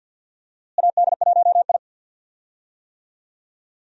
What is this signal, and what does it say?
Morse code, heard as a keyed tone, about 47 wpm, 700 Hz. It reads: AD1I
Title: USA Call Signs - 4 Characters - Repeated 2x - 50wpm